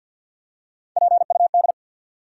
Morse code, heard as a keyed tone, about 53 wpm, 700 Hz. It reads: PUD
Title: USA Call Sign Postfixes - 3 Letters 50wpm